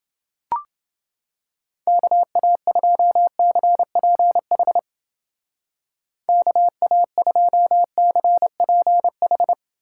Morse code, heard as a keyed tone, about 29 wpm, 700 Hz. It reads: KA2CPH KA2CPH